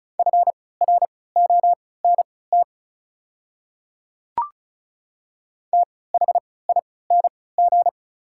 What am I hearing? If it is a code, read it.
Morse code, heard as a keyed tone, about 34 wpm, spaced as for 20 wpm, 690 Hz. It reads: FRONT THING
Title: 3x Character Spacing - Top 500 Words - 35wpm